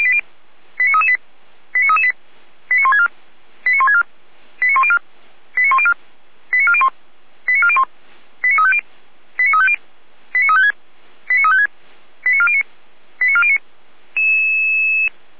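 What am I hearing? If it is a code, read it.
Five-tone selective call (ZVEI), heard as quick runs of five tones, tones of about 70 ms. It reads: 89338 89338 89164 89164 89104 89104 89401 89401 89350 89350 89356 89356 89409 89409
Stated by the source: ZVEI five-tone selective call received by SDR.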